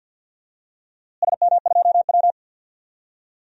Morse code, heard as a keyed tone, about 50 wpm, 700 Hz. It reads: SM2W
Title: DXCC-100 Call Signs - 4 Characters - Repeated 3x - 50wpm